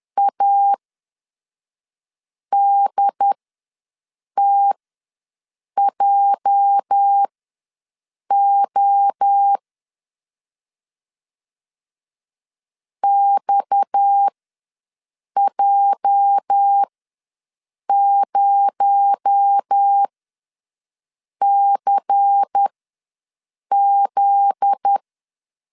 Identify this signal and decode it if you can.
Morse code, heard as a keyed tone, about 10 wpm, spaced as for 5 wpm, 800 Hz. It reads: ADTJO XJ0CZ
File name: LEZIONI / 8 YJ45 / 8 YJ45 ws 10wpm 3843